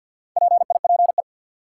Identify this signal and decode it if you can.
Morse code, heard as a keyed tone, about 48 wpm, 700 Hz. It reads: PIPE
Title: Sentences from Top 1500 Words - New Word Drill - 50wpm